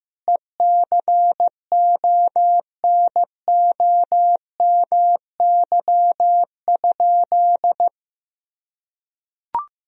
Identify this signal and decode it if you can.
Morse code, heard as a keyed tone, about 15 wpm, 700 Hz. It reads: ECONOMY?
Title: Sentences from Top 1000 Words - Review Part 3 - 15wpm